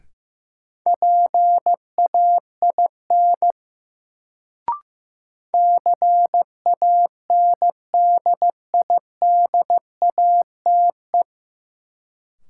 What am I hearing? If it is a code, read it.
Morse code, heard as a keyed tone, about 15 wpm, 700 Hz. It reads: PAIN CANDIDATE
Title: Sentences from Top 1600 Words - New Word Drill Alt - 15wpm